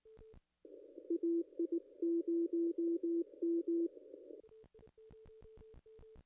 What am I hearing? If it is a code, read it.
Morse code, heard as a keyed tone, about 18 wpm, 340 Hz. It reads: AI0M
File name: contests / ss-cw-k5tr-op-n5rz -0100